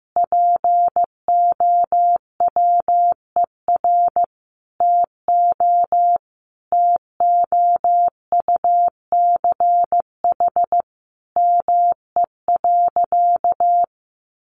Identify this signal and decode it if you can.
Morse code, heard as a keyed tone, about 15 wpm, 700 Hz. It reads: POWER TO TOUCH ME.
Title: Sentences from Top 1200 Words 15wpm